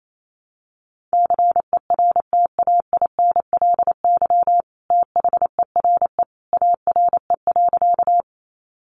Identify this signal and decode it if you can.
Morse code, heard as a keyed tone, about 28 wpm, 700 Hz. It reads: CERTAINLY THERE ARE.
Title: Sentences from Top 900 Words 28wpm